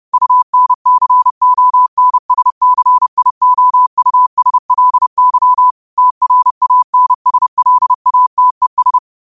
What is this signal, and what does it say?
Morse code, heard as a keyed tone, about 30 wpm, 1000 Hz. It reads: UNCONSCIOUSLY TRANSLATES